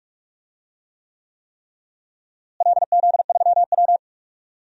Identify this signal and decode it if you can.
Morse code, heard as a keyed tone, about 46 wpm, 700 Hz. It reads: LZ3W
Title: DXCC-100 Call Signs - 4 Characters - Repeated 3x - 45wpm